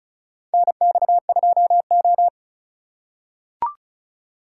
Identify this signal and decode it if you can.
Morse code, heard as a keyed tone, about 36 wpm, 690 Hz. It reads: NX2O
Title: USA Call Signs - 4 Characters - 3x Character Spacing - 35wpm